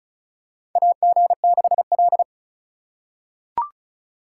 Morse code, heard as a keyed tone, about 36 wpm, 700 Hz. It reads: AG6L
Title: USA Call Signs - 3x Character Spacing - 35wpm